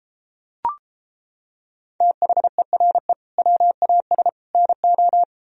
Morse code, heard as a keyed tone, about 29 wpm, 700 Hz. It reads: THERE WAS NO